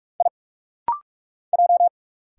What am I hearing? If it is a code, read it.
Morse code, heard as a keyed tone, about 44 wpm, 700 Hz. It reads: I J
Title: Single Letter-Number Alt - Rapid-Fire 45wpm